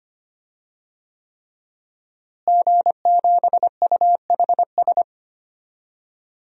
Morse code, heard as a keyed tone, about 25 wpm, 700 Hz. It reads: G7UHS
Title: DXCC-100 Call Signs - 5 Characters - Repeated 3x - 25wpm